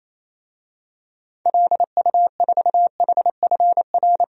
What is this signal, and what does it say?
Morse code, heard as a keyed tone, about 27 wpm, 700 Hz. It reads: LU4HFR
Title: DXCC-100 Call Signs - 6 Characters - 28wpm